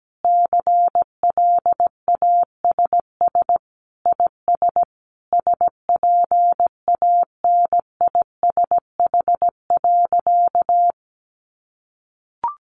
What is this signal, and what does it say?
Morse code, heard as a keyed tone, about 17 wpm, 700 Hz. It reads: CLASS IS SPANISH.
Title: Additional Easy Phrases 17wpm